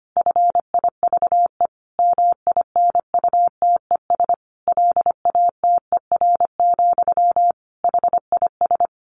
Morse code, heard as a keyed tone, about 25 wpm, 700 Hz. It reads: FIVE MINUTES LATER, HIS
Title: A Sound Of Thunder 25wpm